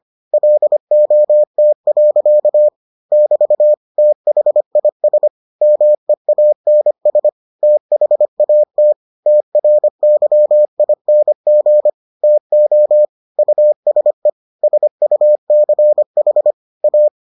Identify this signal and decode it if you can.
Morse code, heard as a keyed tone, about 25 wpm, 600 Hz. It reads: LOT. <BT> THIS MEANS THAT TRYING TO USE SUCH A